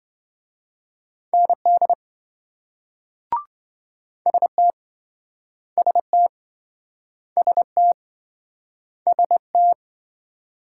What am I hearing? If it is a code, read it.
Morse code, heard as a keyed tone, about 28 wpm, 700 Hz. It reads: ND ST ST ST ST